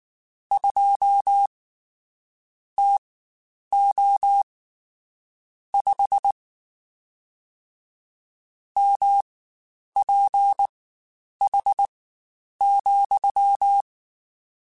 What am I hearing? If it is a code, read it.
Morse code, heard as a keyed tone, about 19 wpm, spaced as for 9 wpm, 800 Hz. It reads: 2 TO 5 MPH,
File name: Spaced Characters / Spaced Characters-mp3s / Weather5 12 sp @18